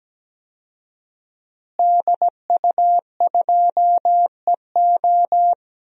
Morse code, heard as a keyed tone, about 17 wpm, 700 Hz. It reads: DU2EO
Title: DXCC-100 Call Signs - 5 Characters - Repeated 3x - 17wpm